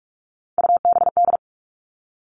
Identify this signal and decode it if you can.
Morse code, heard as a keyed tone, about 44 wpm, 700 Hz. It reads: UBD
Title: All Cut Numbers - 3-Digit 45wpm